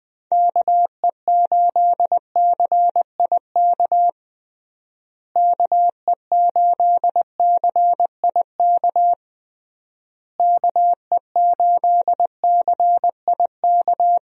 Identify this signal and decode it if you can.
Morse code, heard as a keyed tone, about 20 wpm, 700 Hz. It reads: KE8CIK KE8CIK KE8CIK
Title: USA Call Signs - 6 Characters - Repeated 3x - 20wpm